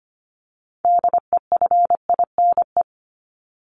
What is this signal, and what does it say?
Morse code, heard as a keyed tone, about 25 wpm, 700 Hz. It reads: DEFINE
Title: ED Suffix - Top 100 Words with Base Words 25wpm